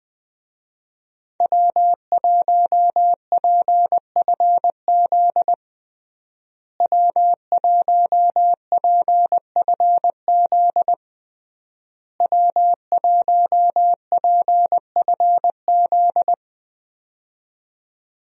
Morse code, heard as a keyed tone, about 20 wpm, 700 Hz. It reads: W1PFZ W1PFZ W1PFZ
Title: USA Call Signs - 5 Characters - Repeated 3x - 20wpm